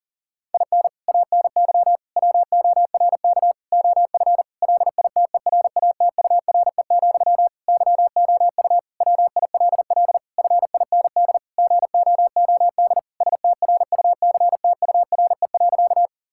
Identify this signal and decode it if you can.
Morse code, heard as a keyed tone, about 40 wpm, 700 Hz. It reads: IN ANY WORK OF LITERATURE, YOU WILL FIND GOOD STRUCTURE.
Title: Sentences from Top 1000 Words - Review Part 6 - 40wpm